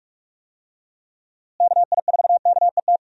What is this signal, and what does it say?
Morse code, heard as a keyed tone, about 41 wpm, 700 Hz. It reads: KI4KET
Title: USA Call Signs - 6 Characters - Repeated 3x - 45wpm